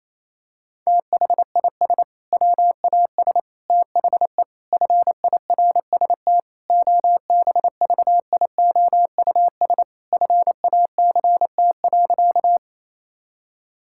Morse code, heard as a keyed tone, about 28 wpm, 700 Hz. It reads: THIS WAS THE FIRST OBVIOUS FACT.
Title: Sentences from Top 1600 Words 28wpm